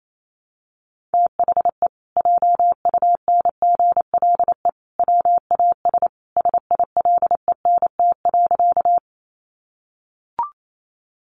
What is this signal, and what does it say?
Morse code, heard as a keyed tone, about 28 wpm, 700 Hz. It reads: THE JUNGLE WAS SILENT.
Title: A Sound Of Thunder 28wpm